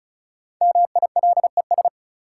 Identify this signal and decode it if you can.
Morse code, heard as a keyed tone, about 36 wpm, 700 Hz. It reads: MILES
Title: Sentences from Top 1000 Words - Review Part 3 - New Word Drill - 35wpm